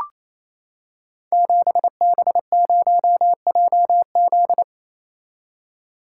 Morse code, heard as a keyed tone, about 27 wpm, 700 Hz. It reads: 7B0JZ
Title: Code Groups - 5 Letter:Numbers - 28wpm